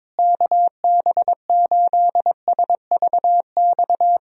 Morse code, heard as a keyed tone, about 21 wpm, 700 Hz. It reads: KB8SVX
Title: USA Call Signs - 6 Characters - Repeated 2x - 22wpm